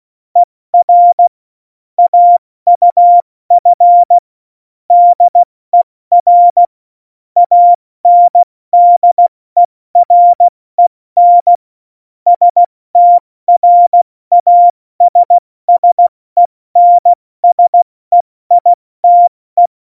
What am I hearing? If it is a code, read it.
Morse code, heard as a keyed tone, about 16 wpm, 700 Hz. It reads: ER AUF DER ANDEREN STRASSENSEITE
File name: Morsekurs / Witze / Witz 31. Oktober 17, Tempo 70